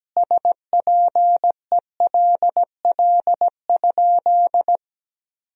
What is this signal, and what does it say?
Morse code, heard as a keyed tone, about 17 wpm, 700 Hz. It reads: SPELL?
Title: Sentences from Top 1000 Words - Review Part 1 - 17wpm